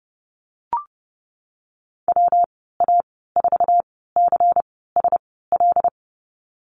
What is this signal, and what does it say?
Morse code, heard as a keyed tone, about 32 wpm, spaced as for 17 wpm, 700 Hz. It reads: WA4CSL
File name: Lesson 19g - Letter w - Call Signs 30wpm